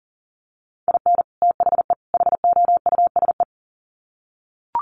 Morse code, heard as a keyed tone, about 40 wpm, 700 Hz. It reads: IN THE HOUSE
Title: Sets of 3 Words 40wpm